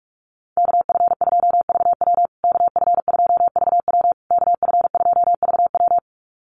Morse code, heard as a keyed tone, about 44 wpm, 700 Hz. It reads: KF2VW KF2VW KF2VW
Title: USA Call Signs Repeated 3x 45wpm